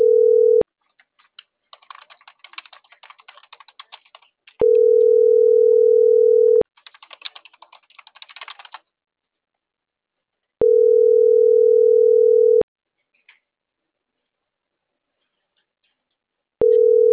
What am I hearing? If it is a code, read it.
Ringback tone (North America), 440 + 480 Hz.